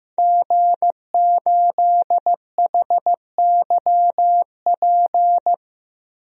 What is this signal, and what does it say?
Morse code, heard as a keyed tone, about 15 wpm, 700 Hz. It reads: G8HYP
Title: DXCC-100 Call Signs - 5 Characters - 15wpm